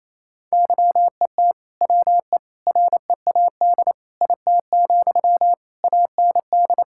Morse code, heard as a keyed tone, about 28 wpm, 700 Hz. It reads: YET WE READ IT, AND
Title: Sentences from Top 1600 Words 28wpm